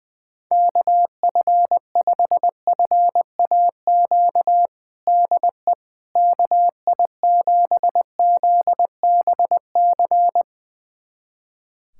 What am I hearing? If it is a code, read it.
Morse code, heard as a keyed tone, about 20 wpm, 700 Hz. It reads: KF5FAQ DE KI7ZBC
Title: USA Callsign DE Callsign 20wpm